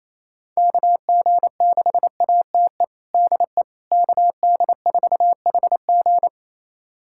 Morse code, heard as a keyed tone, about 28 wpm, 700 Hz. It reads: KG6ATE DE KD4HG
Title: USA Callsign DE Callsign 28wpm